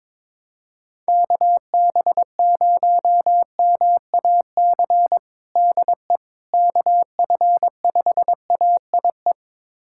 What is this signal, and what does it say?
Morse code, heard as a keyed tone, about 22 wpm, 700 Hz. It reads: KB0MAC DE KF5AIE
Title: USA Callsign DE Callsign - 6 Characters - 22wpm